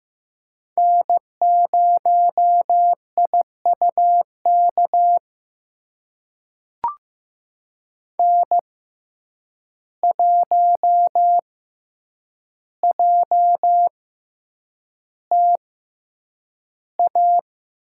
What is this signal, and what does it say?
Morse code, heard as a keyed tone, about 15 wpm, spaced as for 8 wpm, 700 Hz. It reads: N0IUK N1JTA
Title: USA Call Signs - 5 Characters - 6x Character Spacing - 15wpm